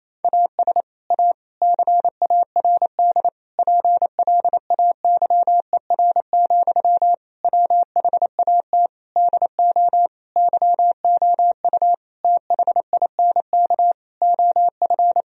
Morse code, heard as a keyed tone, about 28 wpm, 700 Hz. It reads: AS A CARD PLAYER, WHAT DO YOU THINK OF
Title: Sentences from Top 1000 Words - Review Part 4 - 28wpm